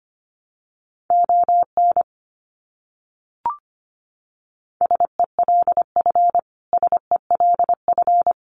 Morse code, heard as a keyed tone, about 25 wpm, 700 Hz. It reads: ON SELF SELF
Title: Top 300 Words Repeated 25wpm